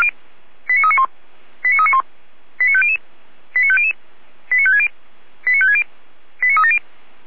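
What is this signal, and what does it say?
Five-tone selective call (ZVEI), heard as quick runs of five tones, tones of about 70 ms. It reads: 89391 89391 89500 89500 89560 89560 89370
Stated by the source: ZVEI five-tone call 89338 received by SDR on 455 channel/frequency.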